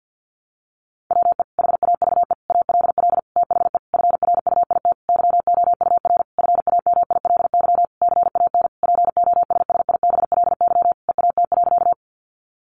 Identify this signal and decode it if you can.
Morse code, heard as a keyed tone, about 50 wpm, 700 Hz. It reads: WE HAVE ALL THE FRUIT YOUR FAMILY CAN POSSIBLY EAT.
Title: Sentences from Top 1300 Words 50wpm